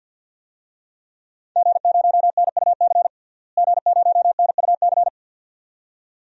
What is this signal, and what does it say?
Morse code, heard as a keyed tone, about 52 wpm, 700 Hz. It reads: G0NUC G0NUC
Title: DXCC-100 Call Signs - 5 Characters - Repeated 2x - 50wpm